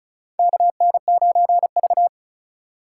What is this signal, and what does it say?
Morse code, heard as a keyed tone, about 36 wpm, 700 Hz. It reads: KN9V
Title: USA Call Signs - 4 Characters - 3x Character Spacing - 35wpm